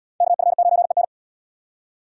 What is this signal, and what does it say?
Morse code, heard as a keyed tone, about 70 wpm, 700 Hz. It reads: KF9EN